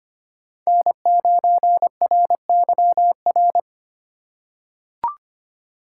Morse code, heard as a keyed tone, about 25 wpm, 700 Hz. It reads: N9RYR
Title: USA Call Signs - 5 Characters - 4x Character Spacing - 25wpm